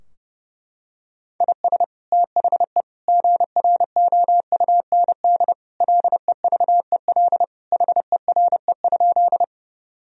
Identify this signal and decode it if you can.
Morse code, heard as a keyed tone, about 31 wpm, 700 Hz. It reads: IS THE GROUND LEVEL HERE?
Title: Sentences from Top 1000 Words - Review Part 4 - 30wpm